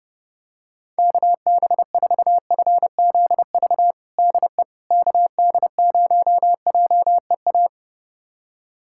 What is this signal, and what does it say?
Morse code, heard as a keyed tone, about 31 wpm, 700 Hz. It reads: KB4FZV DE KD0JEA
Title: USA Callsign DE Callsign - 6 Characters - 30wpm